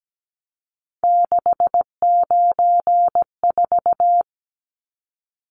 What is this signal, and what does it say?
Morse code, heard as a keyed tone, about 17 wpm, 700 Hz. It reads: 694
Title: Numbers - 3-Digit 17wpm